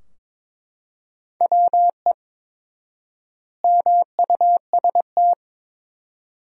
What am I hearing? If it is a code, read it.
Morse code, heard as a keyed tone, about 22 wpm, 700 Hz. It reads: WE MUST
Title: 4x Word-Spacing - Sentences from Top 500 Words - 22wpm